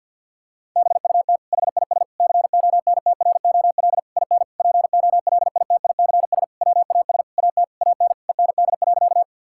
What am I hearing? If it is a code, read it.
Morse code, heard as a keyed tone, about 50 wpm, 700 Hz. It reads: BUT HIS CONTROL IN POLITICS WAS AT AN END.
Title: Sentences from Top 1000 Words - Review Part 6 - 50wpm